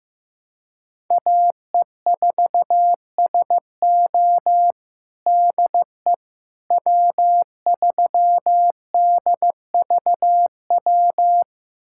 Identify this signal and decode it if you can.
Morse code, heard as a keyed tone, about 15 wpm, 700 Hz. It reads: AE4SO DE W3DVW
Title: USA Callsign DE Callsign - 5 Characters - 15wpm